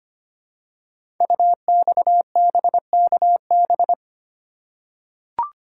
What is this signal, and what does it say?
Morse code, heard as a keyed tone, about 25 wpm, 690 Hz. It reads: UXBKB
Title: Code Groups - 5 Letters - 25wpm